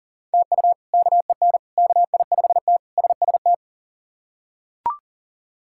Morse code, heard as a keyed tone, about 37 wpm, 700 Hz. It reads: TU KEN KI5T SST
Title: K1USN SST QSO Practice - 40wpm